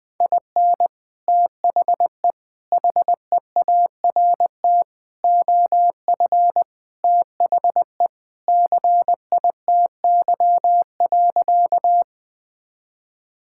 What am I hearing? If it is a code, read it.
Morse code, heard as a keyed tone, about 20 wpm, 700 Hz. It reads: IN THE HEART OF THE CITY.
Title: Sentences with 3-Word Phrases - Spoken Beginning - 20wpm